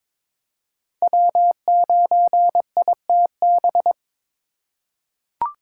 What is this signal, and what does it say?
Morse code, heard as a keyed tone, about 22 wpm, 700 Hz. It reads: W9ITB
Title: USA Call Signs - 4x Character Spacing - 22wpm